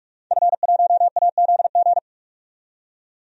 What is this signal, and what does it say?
Morse code, heard as a keyed tone, about 46 wpm, 700 Hz. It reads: F1AZG
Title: DXCC-100 Call Signs - 5 Characters - 45wpm